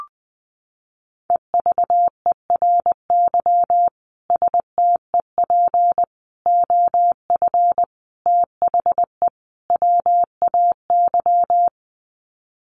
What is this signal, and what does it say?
Morse code, heard as a keyed tone, about 20 wpm, 700 Hz. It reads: EVERY STEP OF THE WAY
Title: Sets of 5 Words Encore2 20wpm